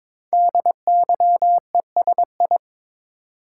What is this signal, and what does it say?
Morse code, heard as a keyed tone, about 22 wpm, 700 Hz. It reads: DYESI
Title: Code Groups - 5 Letters - 22wpm